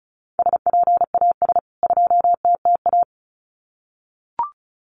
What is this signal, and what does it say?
Morse code, heard as a keyed tone, about 34 wpm, 700 Hz. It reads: SPAS 2TTA
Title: CWT Exchange 35wpm